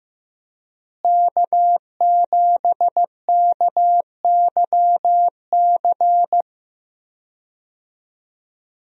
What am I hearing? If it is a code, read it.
Morse code, heard as a keyed tone, about 15 wpm, 700 Hz. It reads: K7KYC
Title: USA Call Signs - 5 Characters - Repeated 2x - 15wpm